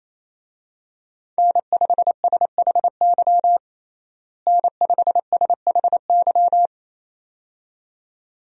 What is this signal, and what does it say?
Morse code, heard as a keyed tone, about 28 wpm, 700 Hz. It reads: N5SHY N5SHY
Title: USA Call Signs - 5 Characters - Repeated 2x - 28wpm